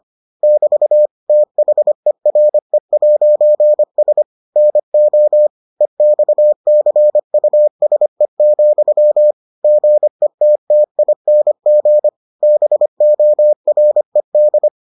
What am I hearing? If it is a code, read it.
Morse code, heard as a keyed tone, about 25 wpm, 600 Hz. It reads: <BT> THERE'S NO EXCUSE, GETTING BORED